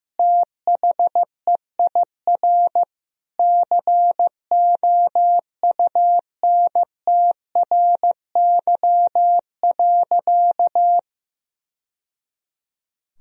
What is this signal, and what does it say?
Morse code, heard as a keyed tone, about 15 wpm, 700 Hz. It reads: THEIR COUNTRY.
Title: Sentences from Top 1000 Words - Review Part 1 - 15wpm